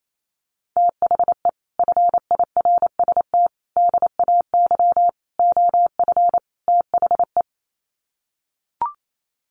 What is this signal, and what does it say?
Morse code, heard as a keyed tone, about 28 wpm, 700 Hz. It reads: THE FIRST DAY OF THE
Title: Sets of 5 Words Encore2 28wpm